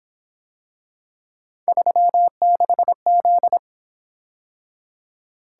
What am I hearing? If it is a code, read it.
Morse code, heard as a keyed tone, about 26 wpm, 700 Hz. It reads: 36Z
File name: Speed-Racing - Three Letter-Number Combinations - 22wpm